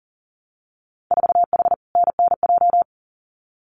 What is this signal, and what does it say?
Morse code, heard as a keyed tone, about 41 wpm, 700 Hz. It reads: 4H NNJ
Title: Winter Field Day Exchanges Encore 40wpm